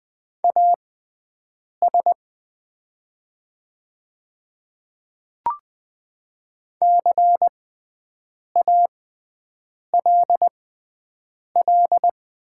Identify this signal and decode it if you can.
Morse code, heard as a keyed tone, about 20 wpm, spaced as for 7 wpm, 700 Hz. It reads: AS CALL
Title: 6x Character Spacing - Top 200 Words - 20wpm